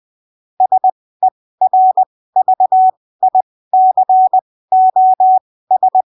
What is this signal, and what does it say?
Morse code, heard as a keyed tone, about 20 wpm, spaced as for 15 wpm, 750 Hz. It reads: SERVICOS
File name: arquivos / texto / texto 12 20ppm